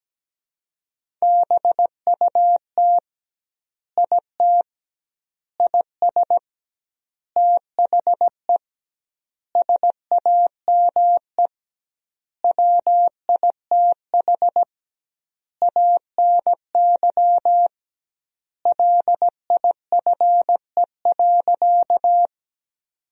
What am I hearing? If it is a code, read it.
Morse code, heard as a keyed tone, about 17 wpm, 700 Hz. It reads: BUT IT IS THE SAME WITH ANY LIFE.
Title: 2x Word-Spacing - Sentences from Top 300 Words - 17wpm